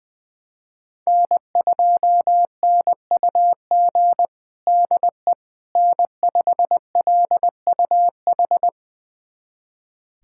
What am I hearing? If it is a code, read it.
Morse code, heard as a keyed tone, about 21 wpm, 700 Hz. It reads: N2NUG DE N5LUH
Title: USA Callsign DE Callsign - 5 Characters - 20wpm